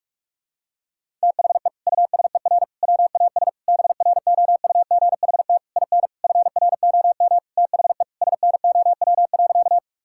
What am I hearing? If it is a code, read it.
Morse code, heard as a keyed tone, about 41 wpm, 700 Hz. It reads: THE USER WAS BROUGHT IN FROM THE SNOW.
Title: Sentences from Top 1000 Words - Review Part 5 - 45wpm